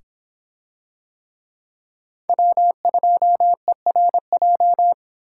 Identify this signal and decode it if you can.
Morse code, heard as a keyed tone, about 24 wpm, 700 Hz. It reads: W2ERJ